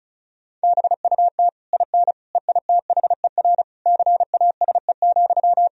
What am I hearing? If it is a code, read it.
Morse code, heard as a keyed tone, about 34 wpm, 700 Hz. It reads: BUT IN EITHER CASE,
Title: Sentences from Top 1000 Words - Review Part 4 - 35wpm